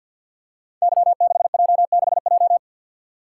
Morse code, heard as a keyed tone, about 52 wpm, 700 Hz. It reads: Y6J6J
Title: Code Groups - 5 Letter:Numbers - 50wpm